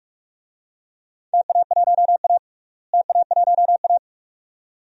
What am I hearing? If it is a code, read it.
Morse code, heard as a keyed tone, about 47 wpm, 700 Hz. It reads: TA1A TA1A
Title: DXCC-100 Call Signs - 4 Characters - Repeated 2x - 45wpm